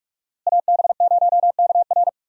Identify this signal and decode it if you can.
Morse code, heard as a keyed tone, about 46 wpm, 700 Hz. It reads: AB0KR
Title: USA Call Signs - 5 Characters - Repeated 2x - 45wpm